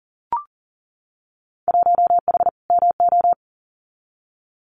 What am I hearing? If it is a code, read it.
Morse code, heard as a keyed tone, about 39 wpm, 700 Hz. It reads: 1H MO
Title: Winter Field Day Exchanges 40wpm